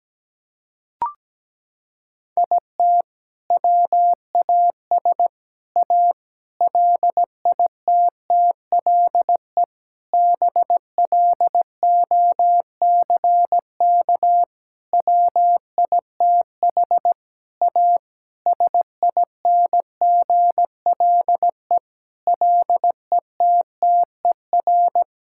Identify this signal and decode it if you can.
Morse code, heard as a keyed tone, about 17 wpm, 700 Hz. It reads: IT WAS A LITTLE BLOCK WITH A SINGLE LETTER
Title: Sentences from Top 1600 Words 17wpm